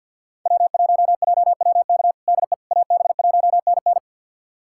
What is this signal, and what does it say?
Morse code, heard as a keyed tone, about 47 wpm, 700 Hz. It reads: W1JWK DE AB1NN
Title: USA Callsign DE Callsign - 5 Characters - 50wpm